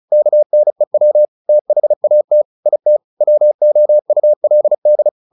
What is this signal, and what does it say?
Morse code, heard as a keyed tone, about 34 wpm, 600 Hz. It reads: KNEW THAT IT WOULD
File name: the old man and the sea 2 35wpm